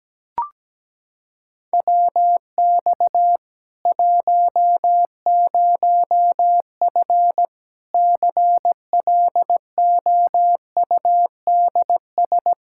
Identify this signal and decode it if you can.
Morse code, heard as a keyed tone, about 17 wpm, 700 Hz. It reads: WX 10F CLOUDS
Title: Practice QSO - Exchange 2 - Info - 17wpm